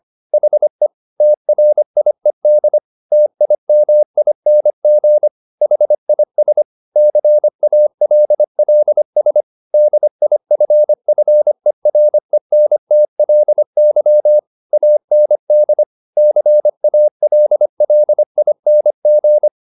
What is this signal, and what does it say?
Morse code, heard as a keyed tone, about 25 wpm, 600 Hz. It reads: HE TRIED TIMING HIS CALLS DIFFERENTLY AND CALLING